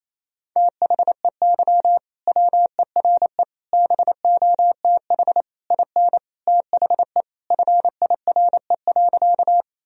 Morse code, heard as a keyed tone, about 27 wpm, 700 Hz. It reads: THEY WERE BOTH IN THE FIRE.
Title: Sentences from Top 1000 Words - Review Part 1 - 28wpm